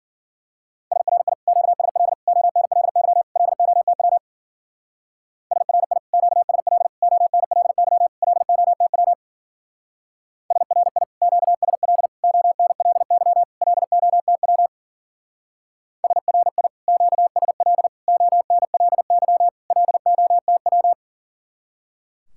Morse code, heard as a keyed tone, about 48 wpm, 700 Hz. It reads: SRI QSL ONLY LOTW SRI QSL ONLY LOTW SRI QSL ONLY LOTW SRI QSL ONLY LOTW
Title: Speed-Racing Common QSO Phrases - 40wpm